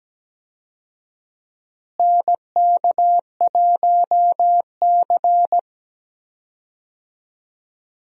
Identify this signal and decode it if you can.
Morse code, heard as a keyed tone, about 17 wpm, 700 Hz. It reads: NK1C
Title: USA Call Signs - 4 Characters - Repeated 2x - 17wpm